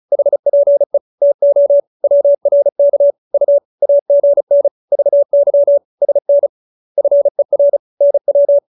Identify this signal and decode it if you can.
Morse code, heard as a keyed tone, about 34 wpm, 570 Hz. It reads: HPE TO WRK U AGN VY SN FER NW